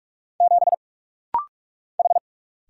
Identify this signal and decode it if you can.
Morse code, heard as a keyed tone, about 44 wpm, 700 Hz. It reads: 7 H
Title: Single Letter-Number Alt - Rapid-Fire 45wpm